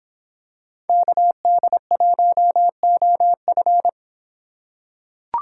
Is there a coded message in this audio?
Yes: Morse code from 0.9 to 3.9 s, heard as a keyed tone, about 26 wpm, 700 Hz. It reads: KD1OF